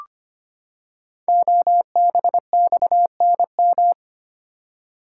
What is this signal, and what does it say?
Morse code, heard as a keyed tone, about 25 wpm, 700 Hz. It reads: OBXNM
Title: Code Groups - 5 Letters - 25wpm